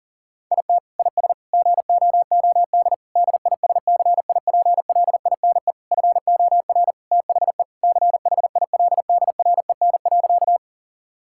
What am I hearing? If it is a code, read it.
Morse code, heard as a keyed tone, about 37 wpm, 690 Hz. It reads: IT IS GOOD DISCIPLINE FOR THE CHILDREN.
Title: Sentences from Top 1000 Words - Review Part 6 - 40wpm